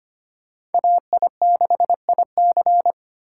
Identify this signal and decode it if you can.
Morse code, heard as a keyed tone, about 31 wpm, 700 Hz. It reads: AI6IC